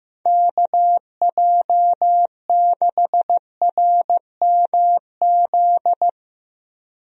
Morse code, heard as a keyed tone, about 15 wpm, 700 Hz. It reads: KJ6RMZ
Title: USA Call Signs - 6 Characters - Repeated 3x - 15wpm